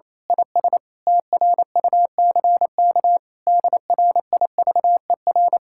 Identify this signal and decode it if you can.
Morse code, heard as a keyed tone, about 26 wpm, 700 Hz. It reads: IS TRUCK DRIVER
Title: Speed-Racing Common QSO Phrases Encore - 28wpm